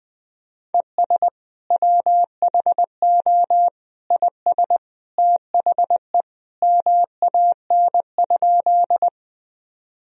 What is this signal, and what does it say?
Morse code, heard as a keyed tone, about 21 wpm, 700 Hz. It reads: ES WHO IS THE MAN?
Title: Sentences from Top 200 Words and Abbreviations 20wpm